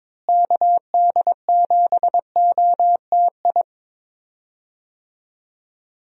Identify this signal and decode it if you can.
Morse code, heard as a keyed tone, about 22 wpm, 700 Hz. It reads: KD7OTI
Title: USA Call Signs - 6 Characters - Repeated 2x - 22wpm